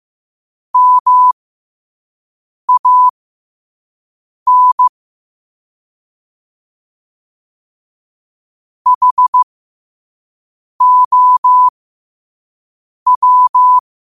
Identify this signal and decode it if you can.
Morse code, heard as a keyed tone, about 15 wpm, spaced as for 5 wpm, 1000 Hz. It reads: MAN HOW